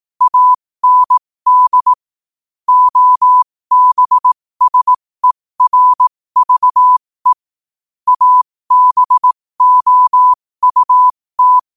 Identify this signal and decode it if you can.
Morse code, heard as a keyed tone, about 18 wpm, spaced as for 15 wpm, 1000 Hz. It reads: AND OBSERVE ABOUT
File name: qotdfiles / 15wpm